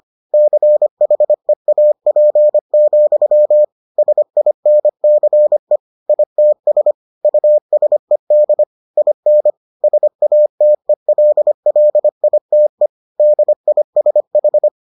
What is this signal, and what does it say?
Morse code, heard as a keyed tone, about 25 wpm, 600 Hz. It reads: CHEAP, SINCE ITS USED IN SATELLITE DISH